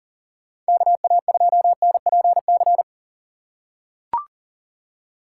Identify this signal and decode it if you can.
Morse code, heard as a keyed tone, about 40 wpm, 700 Hz. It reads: KA2NPC
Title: USA Call Signs - 6 Characters - 4x Character Spacing - 40wpm